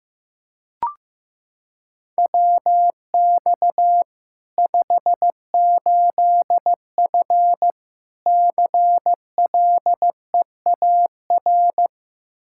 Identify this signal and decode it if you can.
Morse code, heard as a keyed tone, about 15 wpm, 700 Hz. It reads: WX 58F CLEAR
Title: Practice QSO - Exchange 2 - Info - 15wpm